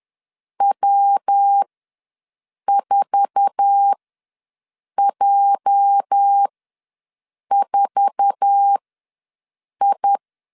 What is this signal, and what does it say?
Morse code, heard as a keyed tone, about 10 wpm, spaced as for 6 wpm, 800 Hz. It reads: W4J4I
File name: LEZIONI / 8 YJ45 / 8 YJ4 ws 10wpm 4020